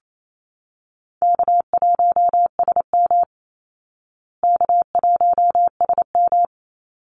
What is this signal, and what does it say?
Morse code, heard as a keyed tone, about 28 wpm, 700 Hz. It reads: K1SM K1SM
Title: Call Signs Repeated 3x - 28wpm